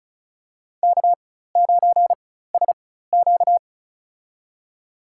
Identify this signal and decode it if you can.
Morse code, heard as a keyed tone, about 36 wpm, spaced as for 16 wpm, 700 Hz. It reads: K9SQ
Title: USA Call Signs - 4 Characters - 4x Character Spacing - 35wpm